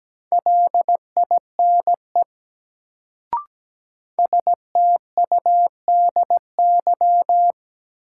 Morse code, heard as a keyed tone, about 17 wpm, 700 Hz. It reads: LINE STUDY
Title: Sets of 5 Words - New Word Drill - 17wpm